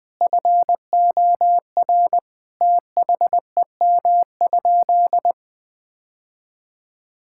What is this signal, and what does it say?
Morse code, heard as a keyed tone, about 20 wpm, 700 Hz. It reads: FOR THEM?
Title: Sentences from Top 1000 Words - Review Part 5 - 20wpm